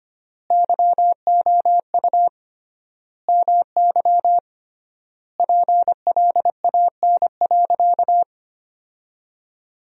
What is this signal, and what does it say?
Morse code, heard as a keyed tone, about 25 wpm, 700 Hz. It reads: YOU MY PLAN.
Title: 3x Word-Spacing - Sentences from Top 500 Words - 25wpm